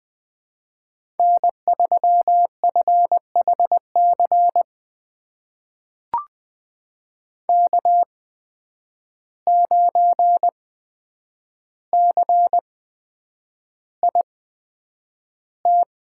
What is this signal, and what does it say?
Morse code, heard as a keyed tone, about 20 wpm, spaced as for 9 wpm, 700 Hz. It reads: N3FHC K9CIT
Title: USA Call Signs - 5 Characters - 8x Character Spacing - 20wpm